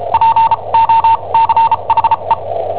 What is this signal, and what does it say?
Morse code, heard as a keyed tone, about 34 wpm, spaced as for 27 wpm, 910 Hz. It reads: POCHE
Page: Cassetta esercizi preparata da I7OHP per IW5ELP